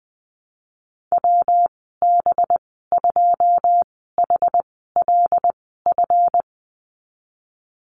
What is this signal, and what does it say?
Morse code, heard as a keyed tone, about 21 wpm, spaced as for 15 wpm, 700 Hz. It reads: WB2HLF
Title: 186 Lesson 30g - Letter b - Call Signs 20wpm